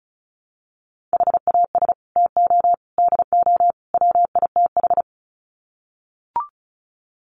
Morse code, heard as a keyed tone, about 34 wpm, 700 Hz. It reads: HAS TO DO WITH
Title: Sets of 4 Words 35wpm